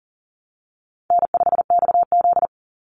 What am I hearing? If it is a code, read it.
Morse code, heard as a keyed tone, about 40 wpm, 700 Hz. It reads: N5XZ
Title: Call Signs Repeated 3x - 40wpm